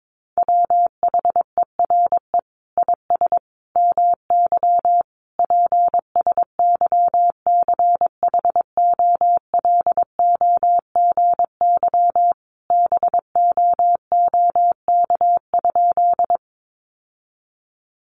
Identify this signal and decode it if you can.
Morse code, heard as a keyed tone, about 22 wpm, 700 Hz. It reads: WHERE IS MY PSYCHOLOGY BOOK?
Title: Sentences from Top 1400 Words 22wpm